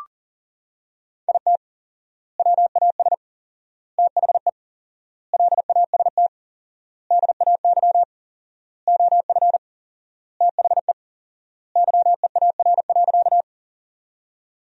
Morse code, heard as a keyed tone, about 40 wpm, 700 Hz. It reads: IT WAS THE LAST DAY OF THE YEAR.
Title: 4x Word-Spacing - Sentences from Top 300 Words - 40wpm